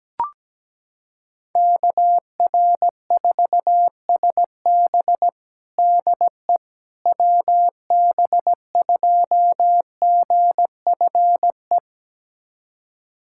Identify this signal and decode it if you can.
Morse code, heard as a keyed tone, about 17 wpm, 700 Hz. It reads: KR4SB DE WB2GFE
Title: USA Callsign DE Callsign 17wpm